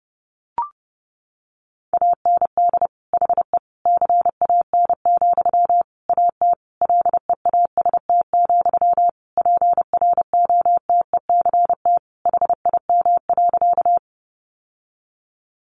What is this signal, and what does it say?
Morse code, heard as a keyed tone, about 28 wpm, 700 Hz. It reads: AND HE CAN, AT LEAST, PROTECT HIM.
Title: Sentences from Top 900 Words 30wpm